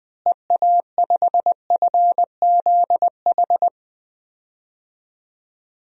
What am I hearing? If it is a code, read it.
Morse code, heard as a keyed tone, about 20 wpm, 700 Hz. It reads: EA5FZH
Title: DXCC-100 Call Signs - 6 Characters - Repeated 2x - 20wpm